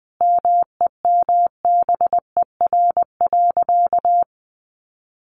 Morse code, heard as a keyed tone, about 20 wpm, 700 Hz. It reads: MEMBER.
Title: Sentences from Top 1200 Words 20wpm